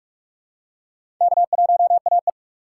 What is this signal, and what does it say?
Morse code, heard as a keyed tone, about 42 wpm, 700 Hz. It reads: K1AE
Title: USA Call Signs - 4 Characters - 3x Character Spacing - 45wpm